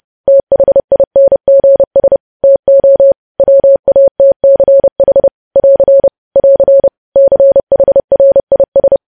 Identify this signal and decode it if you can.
Morse code, heard as a keyed tone, about 30 wpm, 550 Hz. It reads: THINGS TO WATCH <AR> <AR> CHRIS
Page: news headlines in morse code at 30 wpm